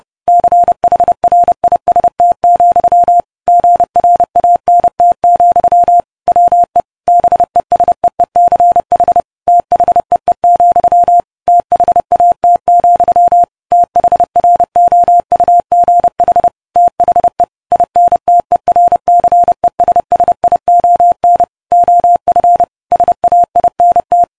Morse code, heard as a keyed tone, about 30 wpm, 700 Hz. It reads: CHRIST, GRANT, WE BESEECH THEE, THAT, THROUGH THE INTERCESSION OF SAINT